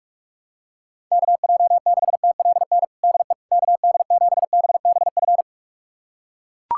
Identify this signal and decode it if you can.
Morse code, heard as a keyed tone, about 42 wpm, 690 Hz. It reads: KJ6TLN DE KD7BBF
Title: USA Callsign DE Callsign - 6 Characters - 45wpm